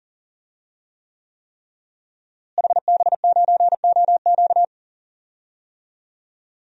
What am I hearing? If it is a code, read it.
Morse code, heard as a keyed tone, about 40 wpm, 700 Hz. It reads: HB9OQ
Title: DXCC-100 Call Signs - 5 Characters - Repeated 3x - 40wpm